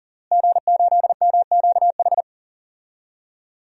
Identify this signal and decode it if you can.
Morse code, heard as a keyed tone, about 37 wpm, 700 Hz. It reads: G8MQH
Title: DXCC-100 Call Signs - 5 Characters - Repeated 3x - 40wpm